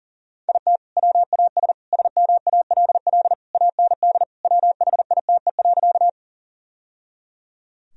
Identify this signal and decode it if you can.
Morse code, heard as a keyed tone, about 37 wpm, 700 Hz. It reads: IT WAS SMALL AND WHITE.
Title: Sentences from Top 1000 Words - Review Part 1 - 40wpm